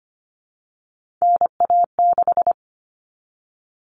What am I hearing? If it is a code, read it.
Morse code, heard as a keyed tone, about 25 wpm, 700 Hz. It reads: NA6
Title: Common Cut Numbers - 3-Digit 25wpm